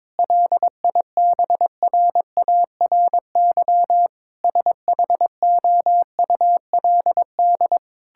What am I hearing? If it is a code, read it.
Morse code, heard as a keyed tone, about 22 wpm, 700 Hz. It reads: LIBRARY SHOULD
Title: Sentences from Top 1000 Words - Review Part 5 - 22wpm